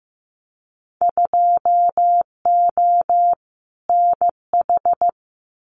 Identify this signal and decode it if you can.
Morse code, heard as a keyed tone, about 15 wpm, 700 Hz. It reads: 2O NH
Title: Winter Field Day Exchanges 15wpm